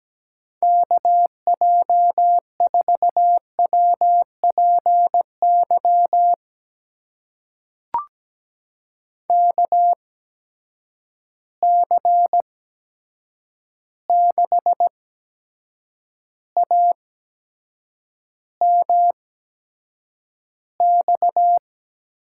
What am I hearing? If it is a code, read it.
Morse code, heard as a keyed tone, about 17 wpm, spaced as for 7 wpm, 700 Hz. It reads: KJ4WPY KC6AMX
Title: USA Call Signs - 8x Character Spacing - 17wpm